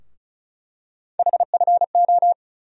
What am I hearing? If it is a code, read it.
Morse code, heard as a keyed tone, about 34 wpm, 700 Hz. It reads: HFO
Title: USA Call Sign Postfixes - 3 Letters 35wpm